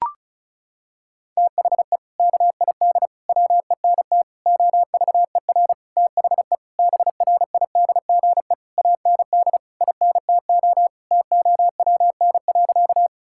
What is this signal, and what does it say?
Morse code, heard as a keyed tone, about 34 wpm, 700 Hz. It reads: THE KID WENT OVER THE BRIDGE AND INTO TOWN.
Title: Sentences from Top 1600 Words 35wpm